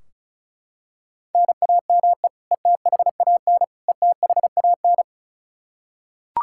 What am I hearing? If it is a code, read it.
Morse code, heard as a keyed tone, about 34 wpm, 700 Hz. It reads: NAME ETHAN ETHAN
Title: Practice QSO - Exchange 1 - Beginning - 35wpm